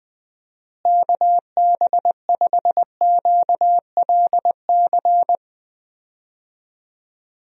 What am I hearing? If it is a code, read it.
Morse code, heard as a keyed tone, about 20 wpm, 700 Hz. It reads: KB5QLC
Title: USA Call Signs - 6 Characters - Repeated 2x - 20wpm